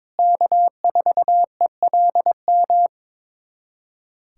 Morse code, heard as a keyed tone, about 22 wpm, 700 Hz. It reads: K4ELM
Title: USA Callsign DE Callsign - 5 Characters - 22wpm